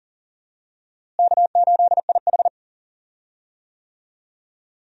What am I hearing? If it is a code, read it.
Morse code, heard as a keyed tone, about 40 wpm, 700 Hz. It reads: K8IH
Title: USA Call Signs - 4 Characters - Repeated 2x - 40wpm